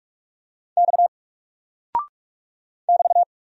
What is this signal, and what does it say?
Morse code, heard as a keyed tone, about 46 wpm, 700 Hz. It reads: X <BT>
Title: Single Character (Letter:Number:Punctuation:Prosign) Alt 45wpm